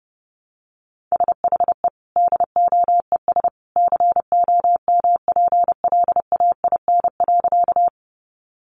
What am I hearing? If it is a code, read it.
Morse code, heard as a keyed tone, about 30 wpm, 700 Hz. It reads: SHE DOES COMPLAIN.
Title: Sentences from Top 1300 Words 30wpm